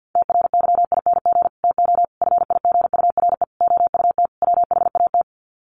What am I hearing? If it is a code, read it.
Morse code, heard as a keyed tone, about 50 wpm, 700 Hz. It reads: TRYING TO FIGURE OUT WHAT
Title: Sets of 5 Words 50wpm